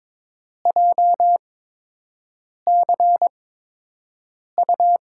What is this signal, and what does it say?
Morse code, heard as a keyed tone, about 22 wpm, spaced as for 6 wpm, 700 Hz. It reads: JCU
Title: USA Call Signs - 5 Characters - 8x Character Spacing - 22wpm